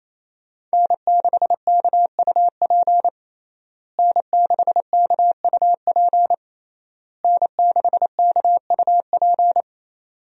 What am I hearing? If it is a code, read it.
Morse code, heard as a keyed tone, about 28 wpm, 700 Hz. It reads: N6KUP N6KUP N6KUP
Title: USA Call Signs - 5 Characters - Repeated 3x - 28wpm